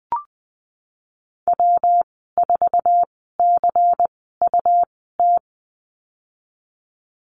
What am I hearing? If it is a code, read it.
Morse code, heard as a keyed tone, about 20 wpm, spaced as for 15 wpm, 700 Hz. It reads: W4CUT
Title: 122 Lesson 20g - Number 3 - Call Signs 20wpm